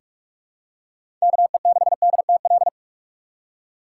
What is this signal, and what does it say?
Morse code, heard as a keyed tone, about 42 wpm, 690 Hz. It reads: KE6DTL
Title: USA Call Signs - 8x Character Spacing - 45wpm